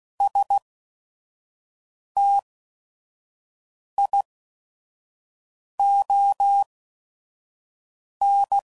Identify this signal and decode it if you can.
Morse code, heard as a keyed tone, about 16 wpm, spaced as for 5 wpm, 800 Hz. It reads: STION
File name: Prefixes / By First Letter / Qs